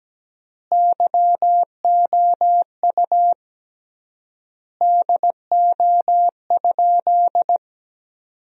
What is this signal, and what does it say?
Morse code, heard as a keyed tone, about 17 wpm, 700 Hz. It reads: YOU DO?
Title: 3x Word-Spacing - Sentences from Top 100 Words - 17wpm